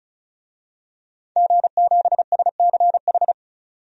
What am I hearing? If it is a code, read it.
Morse code, heard as a keyed tone, about 35 wpm, 700 Hz. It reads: G7SCH
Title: DXCC-100 Call Signs - 5 Characters - 35wpm